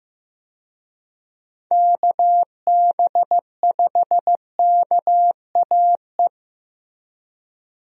Morse code, heard as a keyed tone, about 16 wpm, 700 Hz. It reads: KB5KAE
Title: USA Call Signs - 6 Characters - Repeated 2x - 15wpm